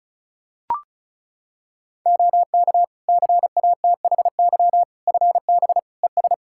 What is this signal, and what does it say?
Morse code, heard as a keyed tone, about 36 wpm, 700 Hz. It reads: OK CATHY FB ES
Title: Practice QSO - Exchange 2 - Info - 35wpm